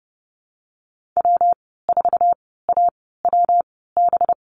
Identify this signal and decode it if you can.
Morse code, heard as a keyed tone, about 30 wpm, spaced as for 17 wpm, 700 Hz. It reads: W4AWB
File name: Lesson 37g - Letter x - Call Signs 30wpm